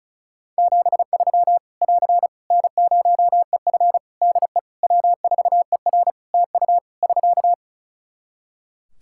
Morse code, heard as a keyed tone, about 36 wpm, 700 Hz. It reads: 73 <AR> N0EF DE W4ER TU <SK>
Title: Practice QSO - Exchange 4 - Ending - 35wpm